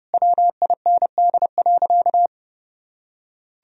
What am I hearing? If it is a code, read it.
Morse code, heard as a keyed tone, about 30 wpm, 700 Hz. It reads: WIND.
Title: Sentences from Top 1000 Words - Review Part 2 - 30wpm